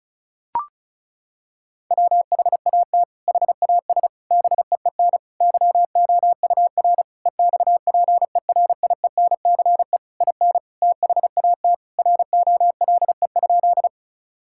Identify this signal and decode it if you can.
Morse code, heard as a keyed tone, about 36 wpm, 700 Hz. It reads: WHAT HAS BEEN YOUR EXPERIENCE IN THAT ROLE?
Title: Sentences from Top 1000 Words - Review Part 6 - 35wpm